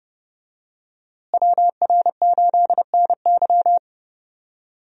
Morse code, heard as a keyed tone, about 32 wpm, 700 Hz. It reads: WR8NY
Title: USA Call Signs - 5 Characters - Repeated 2x - 30wpm